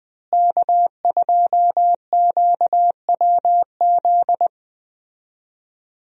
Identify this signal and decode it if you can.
Morse code, heard as a keyed tone, about 21 wpm, 700 Hz. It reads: K2QWZ
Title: USA Call Signs - 5 Characters - Repeated 2x - 20wpm